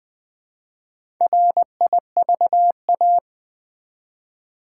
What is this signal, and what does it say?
Morse code, heard as a keyed tone, about 20 wpm, 700 Hz. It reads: RIVA
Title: Build A Word - 1 Letter At A Time - 3 to 5-Letter Words - 20wpm